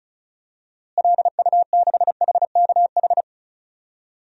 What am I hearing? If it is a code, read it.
Morse code, heard as a keyed tone, about 36 wpm, 700 Hz. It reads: LU6HKH
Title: DXCC-100 Call Signs - 6 Characters - Repeated 3x - 35wpm